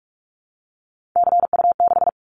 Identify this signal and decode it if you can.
Morse code, heard as a keyed tone, about 46 wpm, 700 Hz. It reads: CU6
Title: Callsign Prefixes for Top 100 DXCC - 45wpm